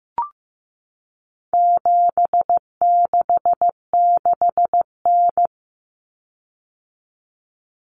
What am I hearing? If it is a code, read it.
Morse code, heard as a keyed tone, about 15 wpm, 700 Hz. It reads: 766N
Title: Common Cut Numbers - 4-Digit 15wpm